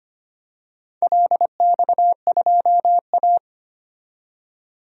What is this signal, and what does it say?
Morse code, heard as a keyed tone, about 25 wpm, 700 Hz. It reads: LX2A
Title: DXCC-100 Call Signs - 4 Characters - 25wpm